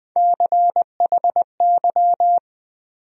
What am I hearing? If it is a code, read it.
Morse code, heard as a keyed tone, about 20 wpm, 700 Hz. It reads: CHY
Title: USA Call Sign Postfixes - 3 Letters 20wpm